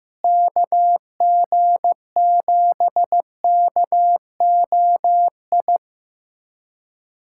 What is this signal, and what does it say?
Morse code, heard as a keyed tone, about 15 wpm, 700 Hz. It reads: KG7KOI
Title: USA Callsign DE Callsign - 6 Characters - 15wpm